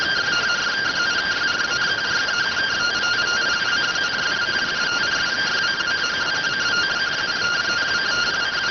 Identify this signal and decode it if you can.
Two-tone teleprinter signal, about 170 Hz shift.